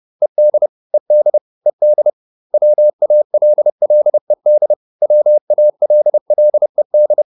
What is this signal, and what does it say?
Morse code, heard as a keyed tone, about 30 wpm, 600 Hz. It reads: ED ED ED WALLED WALLED